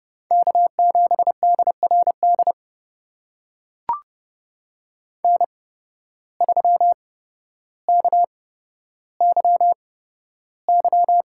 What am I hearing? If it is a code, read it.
Morse code, heard as a keyed tone, about 29 wpm, spaced as for 13 wpm, 700 Hz. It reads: K7DRD N3KYY
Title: USA Call Signs - 5 Characters - 8x Character Spacing - 30wpm